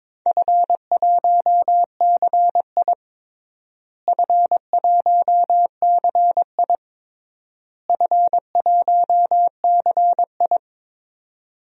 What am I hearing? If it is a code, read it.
Morse code, heard as a keyed tone, about 22 wpm, 700 Hz. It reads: F1CI F1CI F1CI
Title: DXCC-100 Call Signs - 4 Characters - Repeated 3x - 22wpm